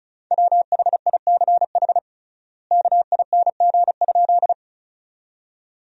Morse code, heard as a keyed tone, about 34 wpm, 700 Hz. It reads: WHICH KING?
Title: 3x Word-Spacing - Sentences from Top 400 Words - 35wpm